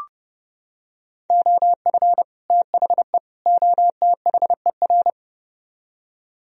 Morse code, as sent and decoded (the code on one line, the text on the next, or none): --- ..-. / - .... / --- - .... .-.
OF THE OTHER